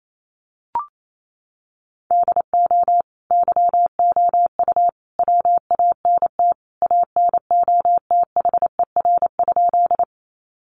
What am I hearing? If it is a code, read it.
Morse code, heard as a keyed tone, about 28 wpm, 700 Hz. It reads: DO YOU WANT ANOTHER?
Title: Sentences from Top 600 Words 28wpm